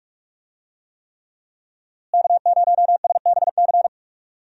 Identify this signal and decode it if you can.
Morse code, heard as a keyed tone, about 44 wpm, 700 Hz. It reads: K0SBC
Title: USA Call Signs - 5 Characters - Repeated 3x - 45wpm